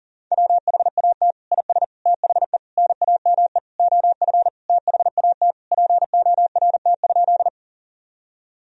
Morse code, heard as a keyed tone, about 40 wpm, 700 Hz. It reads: WHAT IS THE NAME OF THAT PORT?
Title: Sentences from Top 1000 Words - Review Part 1 - 40wpm